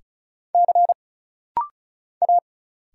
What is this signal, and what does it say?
Morse code, heard as a keyed tone, about 35 wpm, 700 Hz. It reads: C A
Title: Single Letters - Rapid-Fire 35wpm